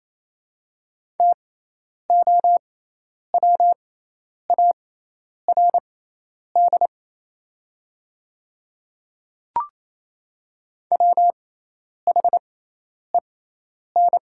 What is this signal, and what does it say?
Morse code, heard as a keyed tone, about 28 wpm, spaced as for 10 wpm, 700 Hz. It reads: TOWARD WHEN
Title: 6x Character Spacing - Top 500 Words - 28wpm